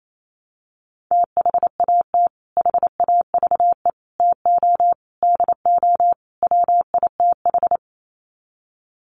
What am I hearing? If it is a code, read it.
Morse code, heard as a keyed tone, about 28 wpm, 700 Hz. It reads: THAT HAVE TO DO WITH
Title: Sets of 5 Words Encore2 28wpm